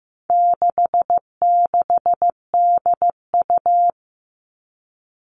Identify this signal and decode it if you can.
Morse code, heard as a keyed tone, about 14 wpm, 700 Hz. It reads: 66DU
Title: All Cut Numbers - 4-Digit 15wpm